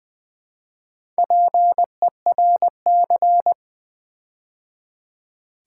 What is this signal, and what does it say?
Morse code, heard as a keyed tone, about 20 wpm, 700 Hz. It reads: PERC
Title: Build A Word - 4 Letters At A Time - 10-Letter Words - 20wpm